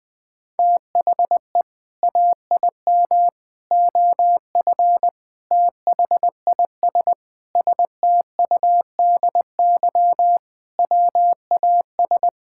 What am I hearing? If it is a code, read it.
Morse code, heard as a keyed tone, about 20 wpm, 700 Hz. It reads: THE AIM OF THIS STUDY WAS
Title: Sentences with 7-Word Phrases - Spoken Beginning - 20wpm